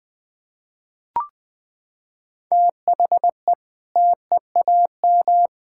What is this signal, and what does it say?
Morse code, heard as a keyed tone, about 19 wpm, 700 Hz. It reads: THE TEAM
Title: Sentences with 6-Word Phrases - 20wpm